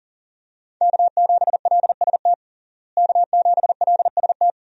Morse code, heard as a keyed tone, about 40 wpm, 700 Hz. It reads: K7LST K7LST
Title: USA Call Signs - 5 Characters - Repeated 2x - 40wpm